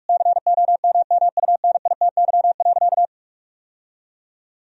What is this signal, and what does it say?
Morse code, heard as a keyed tone, about 41 wpm, 700 Hz. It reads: COMMUNITY.
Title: Sentences from Top 1000 Words - Review Part 4 - 45wpm